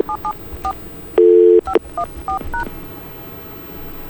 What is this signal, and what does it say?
DTMF, heard as telephone touch tones, tones of about 90 ms. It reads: 774 514#